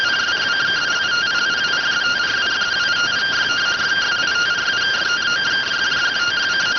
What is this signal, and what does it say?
Two-tone teleprinter signal, about 170 Hz shift.